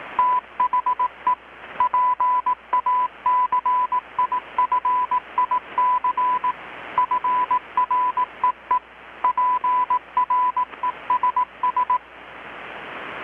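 Morse code, heard as a keyed tone, about 18 wpm, 1000 Hz. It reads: THE PACIFIC FREE PRESS
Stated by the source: Morse from KPH's Night of Nights XX maritime radiotelegraphy event, received in Chiapas, Mexico about 3,800 km away on a Kenwood R-600 with a grounded 10 m long-wire antenna.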